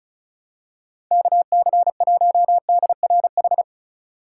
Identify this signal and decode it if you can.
Morse code, heard as a keyed tone, about 34 wpm, 690 Hz. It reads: KC1DRH
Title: USA Call Signs - 6 Characters - 3x Character Spacing - 35wpm